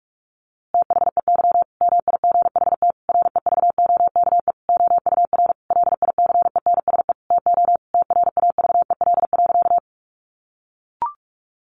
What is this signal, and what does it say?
Morse code, heard as a keyed tone, about 44 wpm, 700 Hz. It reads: THEY MIGHT REVOKE OUR LICENSE TO TRAVEL.
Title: A Sound Of Thunder 45wpm